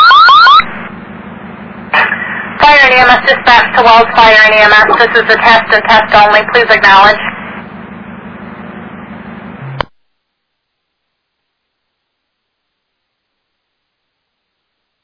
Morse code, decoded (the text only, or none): none